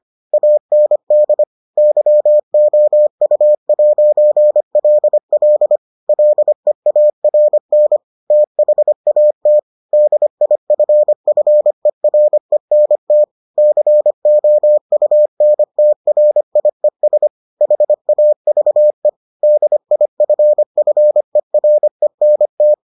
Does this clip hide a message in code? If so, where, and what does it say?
Yes: Morse code from 0.3 to 22.7 s, heard as a keyed tone, about 25 wpm, 600 Hz. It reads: AND YOU'LL LEARN THAT DIFFERENT COUNTRIES HAVE DIFFERENT